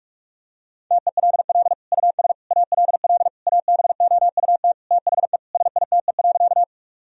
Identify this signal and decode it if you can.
Morse code, heard as a keyed tone, about 46 wpm, 700 Hz. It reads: TELL US ALL ABOUT THE SITE.
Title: Sentences from Top 1000 Words - Review Part 4 - 45wpm